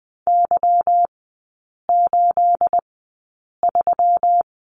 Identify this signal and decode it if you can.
Morse code, heard as a keyed tone, about 20 wpm, spaced as for 8 wpm, 690 Hz. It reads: Y83
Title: 212 Lesson 34e - Number 8 - Three Characters 20wpm